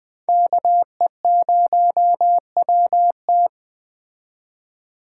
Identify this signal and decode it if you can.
Morse code, heard as a keyed tone, about 21 wpm, 700 Hz. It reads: KE0WT